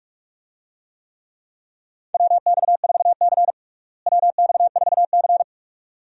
Morse code, heard as a keyed tone, about 42 wpm, 700 Hz. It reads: WX4C WX4C
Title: USA Call Signs - 4 Characters - Repeated 2x - 45wpm